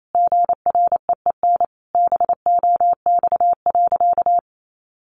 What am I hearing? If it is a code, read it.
Morse code, heard as a keyed tone, about 28 wpm, 700 Hz. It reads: GREEN BOX.
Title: Sentences from Top 500 Words 28wpm